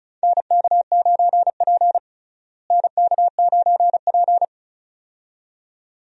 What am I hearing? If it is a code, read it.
Morse code, heard as a keyed tone, about 36 wpm, 700 Hz. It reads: NK9P NK9P
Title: USA Call Signs - 4 Characters - Repeated 2x - 35wpm